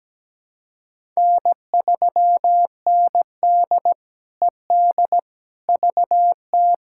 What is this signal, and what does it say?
Morse code, heard as a keyed tone, about 17 wpm, 700 Hz. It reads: N3ND ED VT
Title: K1USN SST QSO Practice - 17wpm